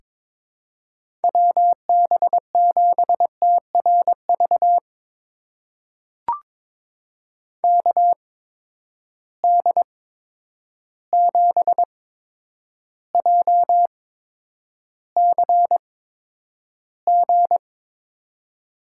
Morse code, heard as a keyed tone, about 22 wpm, spaced as for 9 wpm, 700 Hz. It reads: WB7TRV KD7JCG
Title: USA Call Signs - 6 Characters - 8x Character Spacing - 22wpm